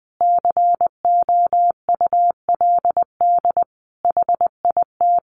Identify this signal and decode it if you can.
Morse code, heard as a keyed tone, about 20 wpm, 700 Hz. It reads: COULD HIT
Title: Sentences from Top 1200 Words 20wpm